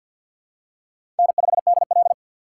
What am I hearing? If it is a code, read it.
Morse code, heard as a keyed tone, about 48 wpm, 700 Hz. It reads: N5DL
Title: USA Call Signs - 3x Character Spacing - 50wpm